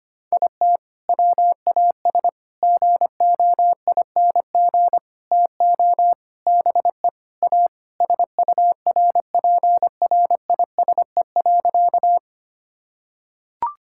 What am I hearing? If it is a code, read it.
Morse code, heard as a keyed tone, about 25 wpm, 700 Hz. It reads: IT WAS GOING TO BE A SURPRISE.
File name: practice / 25wpm / Sentences with 4-Word Phrases - 25wpm